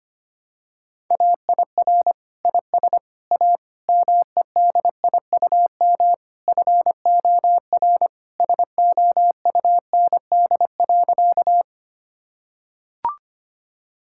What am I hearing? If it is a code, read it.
Morse code, heard as a keyed tone, about 25 wpm, 700 Hz. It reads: AIR IS A MEDIUM FOR SOUND.
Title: Sentences from Top 1000 Words - Review Part 6 - 25wpm